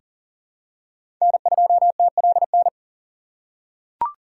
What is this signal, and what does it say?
Morse code, heard as a keyed tone, about 40 wpm, 700 Hz. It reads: N2TLN
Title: USA Call Signs - 2x Character Spacing - 40wpm